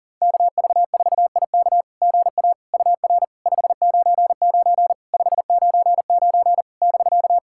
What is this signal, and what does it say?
Morse code, heard as a keyed tone, about 40 wpm, 700 Hz. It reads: KV4IK GA UR 599 599 <BK>
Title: SOTA QSO Practice 40wpm